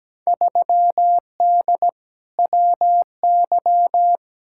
Morse code, heard as a keyed tone, about 17 wpm, 700 Hz. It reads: 3D WY
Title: Field Day Exchanges Encore - 17wpm